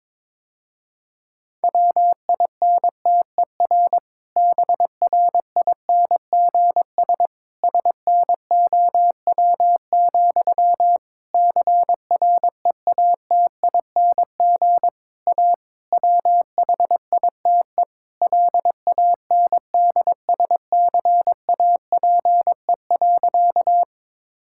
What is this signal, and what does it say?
Morse code, heard as a keyed tone, about 22 wpm, 700 Hz. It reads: WINTER BRINGS SNOW, CREATING A WHITE LANDSCAPE.
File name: Difficult Sentences with Hints - 22wpm